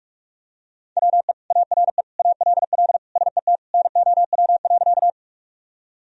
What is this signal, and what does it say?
Morse code, heard as a keyed tone, about 42 wpm, 700 Hz. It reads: WE ARE ALL SET NOW.
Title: Sentences from Top 1000 Words - Review Part 1 - 45wpm